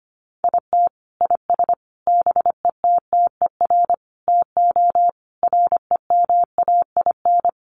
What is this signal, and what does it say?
Morse code, heard as a keyed tone, about 25 wpm, 700 Hz. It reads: IT IS BETTER TO REMAIN
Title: Sentences from Top 1200 Words 25wpm